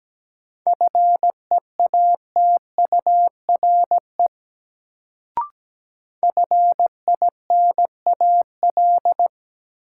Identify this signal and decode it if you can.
Morse code, heard as a keyed tone, about 17 wpm, 700 Hz. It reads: FEATURE FINAL
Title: Sentences from Top 800 Words - New Word Drill - 17wpm